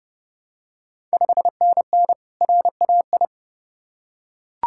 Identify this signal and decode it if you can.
Morse code, heard as a keyed tone, about 29 wpm, 700 Hz. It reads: 5NN RAI
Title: Ontario QSO Party Exchange 30wpm